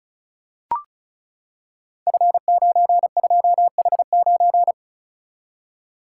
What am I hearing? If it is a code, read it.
Morse code, heard as a keyed tone, about 36 wpm, 700 Hz. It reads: F92H9
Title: Code Groups - 5 Letter:Numbers - 35wpm